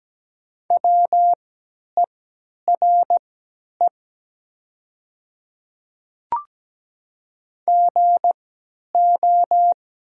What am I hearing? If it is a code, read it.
Morse code, heard as a keyed tone, about 17 wpm, spaced as for 10 wpm, 700 Hz. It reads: WERE GO
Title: 3x Character Spacing - Top 100 Words - 17wpm